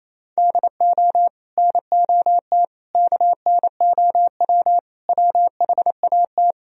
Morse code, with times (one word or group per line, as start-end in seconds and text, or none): 0.4-1.3 DO
1.6-2.6 NOT
3.0-4.8 KNOW
5.1-6.5 WHAT